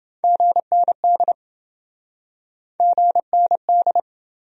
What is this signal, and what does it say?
Morse code, heard as a keyed tone, about 28 wpm, 700 Hz. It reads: GND GND